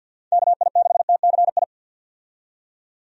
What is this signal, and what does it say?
Morse code, heard as a keyed tone, about 51 wpm, 700 Hz. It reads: KI6TCI
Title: USA Call Signs - 6 Characters - 50wpm